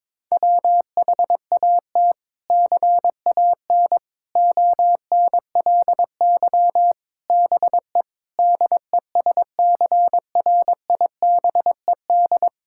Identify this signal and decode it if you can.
Morse code, heard as a keyed tone, about 22 wpm, 700 Hz. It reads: WHAT CAN ONLY BE DESCRIBED
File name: practice / 22wpm / Sets of 6 Words - 22wpm